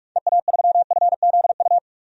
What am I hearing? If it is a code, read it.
Morse code, heard as a keyed tone, about 44 wpm, 700 Hz. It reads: EA3FZU
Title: DXCC-100 Call Signs - 6 Characters - 45wpm